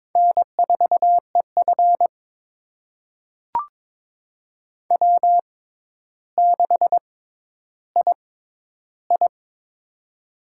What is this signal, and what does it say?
Morse code, heard as a keyed tone, about 22 wpm, spaced as for 11 wpm, 700 Hz. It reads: N4EF W6II
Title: USA Call Signs - 4 Characters - 6x Character Spacing - 22wpm